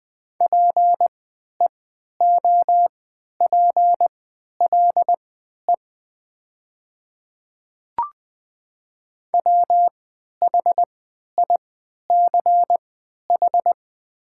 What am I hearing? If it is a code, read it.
Morse code, heard as a keyed tone, about 20 wpm, spaced as for 11 wpm, 700 Hz. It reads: PEOPLE WHICH
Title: 3x Character Spacing - Top 100 Words - 20wpm